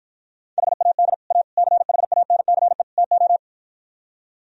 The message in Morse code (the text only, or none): HAD A CHANCE TO